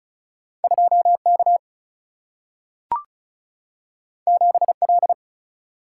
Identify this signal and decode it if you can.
Morse code, heard as a keyed tone, about 36 wpm, 700 Hz. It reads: 2K 7L
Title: Single-Letter Single-Number 35wpm